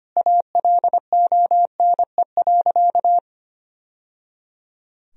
Morse code, heard as a keyed tone, about 25 wpm, 700 Hz. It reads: ALONE.
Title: Sentences from Top 1000 Words - Review Part 3 - 25wpm